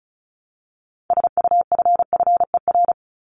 Morse code, heard as a keyed tone, about 34 wpm, 700 Hz. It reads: SUFFER
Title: ING Suffix - Top 100 Words with Base Words 35wpm